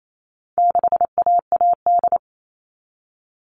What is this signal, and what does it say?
Morse code, heard as a keyed tone, about 28 wpm, 700 Hz. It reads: 6AAD
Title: All Cut Numbers - 4-Digit 28wpm